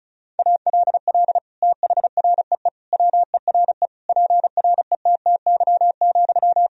Morse code, heard as a keyed tone, about 36 wpm, 700 Hz. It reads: ALL THREE WERE PRETTY,
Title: Sentences from Top 1000 Words - Review Part 4 - 35wpm